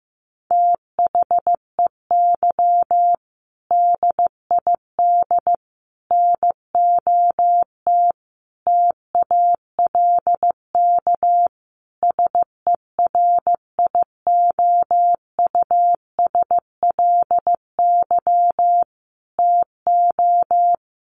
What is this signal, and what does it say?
Morse code, heard as a keyed tone, about 15 wpm, 700 Hz. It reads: THEY DID NOT TALK SERIOUSLY TO
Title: Sentences from Top 1200 Words 15wpm